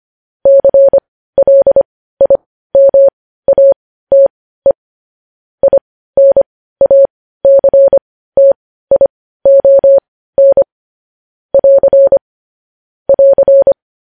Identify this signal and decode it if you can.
Morse code, heard as a keyed tone, about 25 wpm, spaced as for 15 wpm, 550 Hz. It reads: CLIMATE INACTION <AR> <AR>
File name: podcast / News headlines-1x-25x15